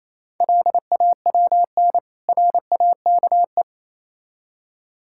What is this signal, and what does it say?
Morse code, heard as a keyed tone, about 28 wpm, 700 Hz. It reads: LAWN RAKE